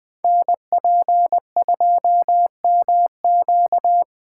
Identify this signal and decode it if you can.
Morse code, heard as a keyed tone, about 20 wpm, 700 Hz. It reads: NP2MQ
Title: Code Groups - 5 Letter:Numbers - 20wpm